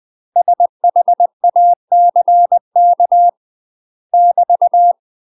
Morse code, heard as a keyed tone, about 20 wpm, 700 Hz. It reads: SHACK <BT>
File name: UK MASTER QSOs TRAINING / 20WPM / UK MASTER QSOs 0012